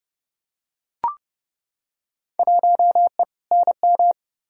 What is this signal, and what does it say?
Morse code, heard as a keyed tone, about 31 wpm, 700 Hz. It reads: 1E NM
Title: Field Day Exchanges - 30wpm